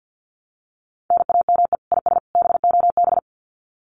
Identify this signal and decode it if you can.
Morse code, heard as a keyed tone, about 50 wpm, 700 Hz. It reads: NAME IS BOB
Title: Common QSO Phrases 50wpm